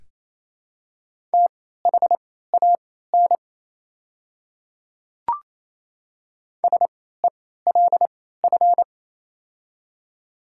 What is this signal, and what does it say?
Morse code, heard as a keyed tone, about 28 wpm, spaced as for 16 wpm, 700 Hz. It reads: THAN SELF
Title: 3x Character Spacing - Top 200 Words - 28wpm